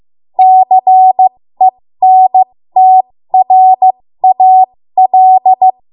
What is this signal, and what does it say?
Morse code, heard as a keyed tone, about 15 wpm, spaced as for 13 wpm, 750 Hz. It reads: CENTRAL
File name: Morse / 13WPM